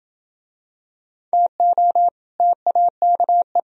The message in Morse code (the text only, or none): TO TAKE